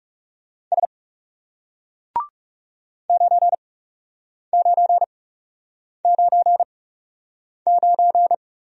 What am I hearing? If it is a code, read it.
Morse code, heard as a keyed tone, about 39 wpm, spaced as for 8 wpm, 700 Hz. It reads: S 9999